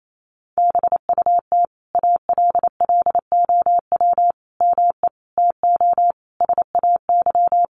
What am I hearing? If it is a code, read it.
Morse code, heard as a keyed tone, about 28 wpm, 700 Hz. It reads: BUT ALLOW ME TO SAY
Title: Sentences from Top 700 Words 28wpm